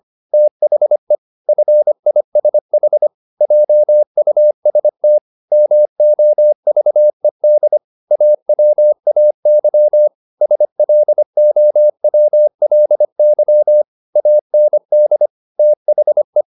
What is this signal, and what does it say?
Morse code, heard as a keyed tone, about 25 wpm, 600 Hz. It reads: THE FISH JUST MOVED AWAY SLOWLY AND THE
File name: the old man and the sea part 1 25wpm